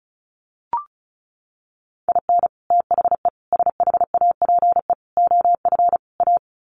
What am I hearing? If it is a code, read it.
Morse code, heard as a keyed tone, about 34 wpm, 700 Hz. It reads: IN THE SHAPE OF A
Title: Sets of 5 Words Encore 35wpm